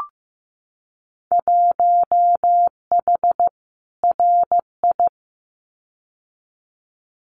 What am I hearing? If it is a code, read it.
Morse code, heard as a keyed tone, about 15 wpm, 700 Hz. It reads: 1H RI
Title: Winter Field Day Exchanges 15wpm